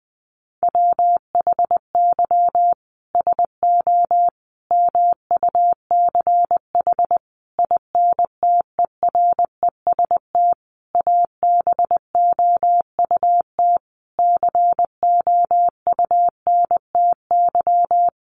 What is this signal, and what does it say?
Morse code, heard as a keyed tone, about 20 wpm, 700 Hz. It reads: WHY SO MUCH INTEREST ABOUT COUNTY
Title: Sentences from Top 1200 Words 20wpm